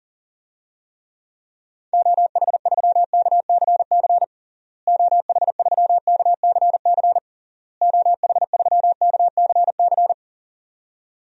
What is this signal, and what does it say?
Morse code, heard as a keyed tone, about 40 wpm, 700 Hz. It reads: OH3KCC OH3KCC OH3KCC
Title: DXCC-100 Call Signs - 6 Characters - Repeated 3x - 40wpm